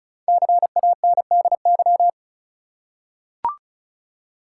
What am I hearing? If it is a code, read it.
Morse code, heard as a keyed tone, about 36 wpm, 700 Hz. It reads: CANDY
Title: Five Letter Words - 35wpm